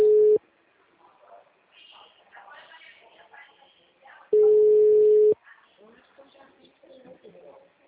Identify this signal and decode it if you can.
Ringback tone (Europe, 425 Hz), with speech.